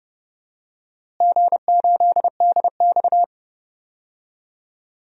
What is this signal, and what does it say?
Morse code, heard as a keyed tone, about 30 wpm, 700 Hz. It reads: G8DX
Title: DXCC-100 Call Signs - 4 Characters - Repeated 2x - 30wpm